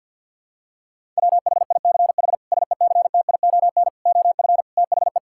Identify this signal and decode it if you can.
Morse code, heard as a keyed tone, about 51 wpm, 690 Hz. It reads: WHICH SECTION OF THE
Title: Sentences from Top 1000 Words - Review Part 5 - 50wpm